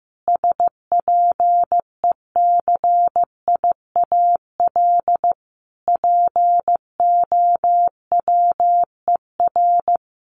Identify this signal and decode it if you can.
Morse code, heard as a keyed tone, about 15 wpm, 700 Hz. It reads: SPECIAL POWER
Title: Sentences from Top 1400 Words 15wpm